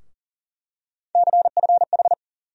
Morse code, heard as a keyed tone, about 40 wpm, 700 Hz. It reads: CFH
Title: USA Call Sign Postfixes - 3 Letters 40wpm